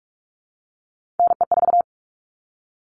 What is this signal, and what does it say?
Morse code, heard as a keyed tone, about 44 wpm, 700 Hz. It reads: NE4
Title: All Cut Numbers - 3-Digit 45wpm